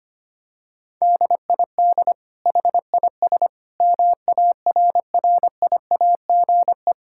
Morse code, heard as a keyed tone, about 25 wpm, 700 Hz. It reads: DID HIS MARRIAGE
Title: Sentences from Top 1000 Words - Review Part 5 - 25wpm